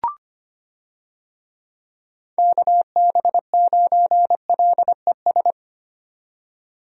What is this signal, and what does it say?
Morse code, heard as a keyed tone, about 25 wpm, 700 Hz. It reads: KB9LES
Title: USA Call Signs - 6 Characters - 25wpm